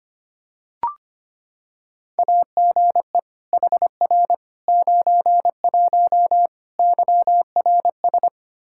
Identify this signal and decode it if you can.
Morse code, heard as a keyed tone, about 25 wpm, 700 Hz. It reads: AGE HR 91 YRS
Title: K1USN SST QSO Practice 25wpm